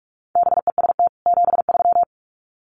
Morse code, heard as a keyed tone, about 44 wpm, 700 Hz. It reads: BEST 73
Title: Common QSO Phrases 45wpm